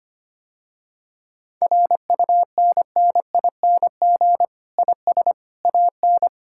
Morse code, heard as a keyed tone, about 25 wpm, 700 Hz. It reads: RUNNING IS AN
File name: practice / 25wpm / Difficult Sentences with Hints - 25wpm